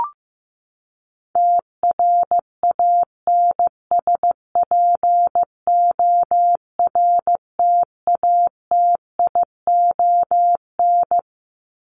Morse code, heard as a keyed tone, about 14 wpm, 700 Hz. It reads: TRANSPORTATION
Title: Top 1501-1750 Words 15wpm